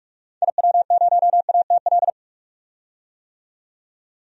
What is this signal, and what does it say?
Morse code, heard as a keyed tone, about 41 wpm, 700 Hz. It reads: IW0ATL
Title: DXCC-100 Call Signs - 6 Characters - 45wpm